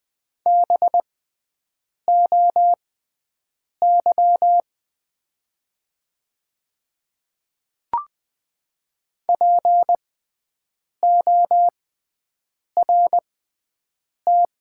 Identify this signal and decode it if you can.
Morse code, heard as a keyed tone, about 20 wpm, spaced as for 7 wpm, 700 Hz. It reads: BOY PORT
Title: 6x Character Spacing - Top 300 Words - 20wpm